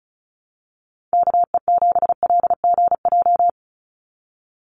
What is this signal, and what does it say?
Morse code, heard as a keyed tone, about 34 wpm, 700 Hz. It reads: KE7LGJ
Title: USA Call Signs 35wpm